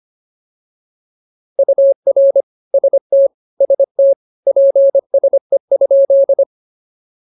Morse code, heard as a keyed tone, about 24 wpm, 550 Hz. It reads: UR ST ST PSE?